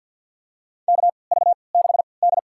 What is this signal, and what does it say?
Morse code, heard as a keyed tone, about 46 wpm, spaced as for 27 wpm, 700 Hz. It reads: KV6D
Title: USA Call Signs - 4 Characters - 3x Character Spacing - 50wpm